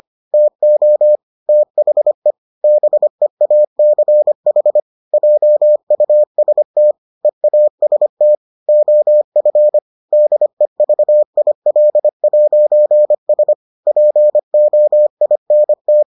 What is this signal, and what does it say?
Morse code, heard as a keyed tone, about 25 wpm, 600 Hz. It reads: TO THE BEACH JUST EAST OF DEVIL'S POINT